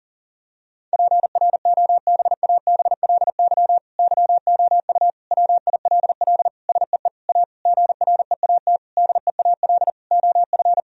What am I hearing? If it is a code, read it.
Morse code, heard as a keyed tone, about 40 wpm, 700 Hz. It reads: PROBABLY YOU WILL SEE A GREAT DEAL OF
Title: Sentences from Top 1000 Words - Review Part 4 - 40wpm